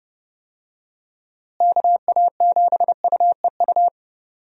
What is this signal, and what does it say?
Morse code, heard as a keyed tone, about 30 wpm, 700 Hz. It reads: KA7UEU
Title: USA Call Signs - 6 Characters - Repeated 2x - 30wpm